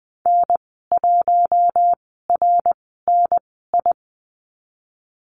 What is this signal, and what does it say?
Morse code, heard as a keyed tone, about 20 wpm, spaced as for 15 wpm, 700 Hz. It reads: N1RNI
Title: 173 Lesson 28g - Number 9 - Call Signs 20wpm